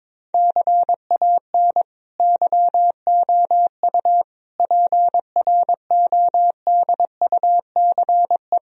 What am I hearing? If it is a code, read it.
Morse code, heard as a keyed tone, about 22 wpm, 700 Hz. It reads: CAN YOU PRODUCE
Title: Sentences from Top 1000 Words - Review Part 3 - 22wpm